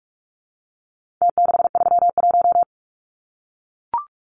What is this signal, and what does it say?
Morse code, heard as a keyed tone, about 44 wpm, 700 Hz. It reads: T631
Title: Numbers - Contest Style - 4-Digit 45wpm